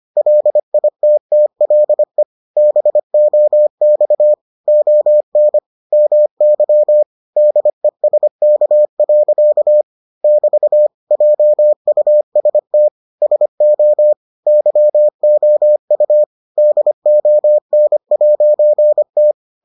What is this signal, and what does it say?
Morse code, heard as a keyed tone, about 25 wpm, 600 Hz. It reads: LITTLE BOX ON MY DESK. <BT> JUST SO YOU DON'T